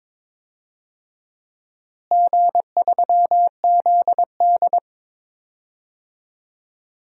Morse code, heard as a keyed tone, about 22 wpm, 700 Hz. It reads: G3ZD
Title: DXCC-100 Call Signs - 4 Characters - Repeated 3x - 22wpm